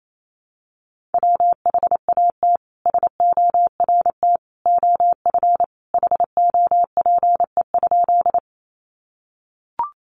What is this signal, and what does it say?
Morse code, heard as a keyed tone, about 28 wpm, 700 Hz. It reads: WHAT SORT OF HOPE?
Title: Sentences from Top 1300 Words 28wpm